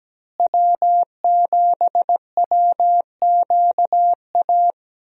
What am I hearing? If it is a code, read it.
Morse code, heard as a keyed tone, about 17 wpm, 700 Hz. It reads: W7WQA
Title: USA Call Signs - 5 Characters - Repeated 3x - 17wpm